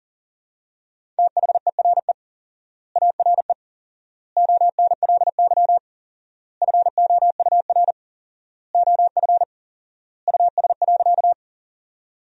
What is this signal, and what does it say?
Morse code, heard as a keyed tone, about 37 wpm, 700 Hz. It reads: THERE ARE ONLY FOUR OF US.
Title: 4x Word-Spacing - Sentences from Top 300 Words - 40wpm